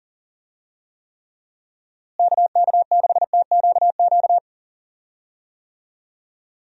Morse code, heard as a keyed tone, about 39 wpm, 700 Hz. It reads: KK6TQQ
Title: USA Call Signs - 6 Characters - Repeated 3x - 40wpm